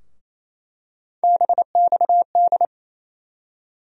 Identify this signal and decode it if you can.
Morse code, heard as a keyed tone, about 28 wpm, 700 Hz. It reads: BXD
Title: USA Call Sign Postfixes - 3 Letters 28wpm